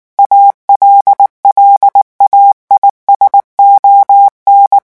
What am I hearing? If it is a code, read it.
Morse code, heard as a keyed tone, about 19 wpm, 800 Hz. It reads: ALLAIS ON
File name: morse mp3 / revision9 citations humoristiques vitesse 18 mots minute